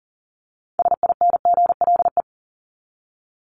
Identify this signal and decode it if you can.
Morse code, heard as a keyed tone, about 40 wpm, 700 Hz. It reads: SINGLE
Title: Top 501-750 Words 40wpm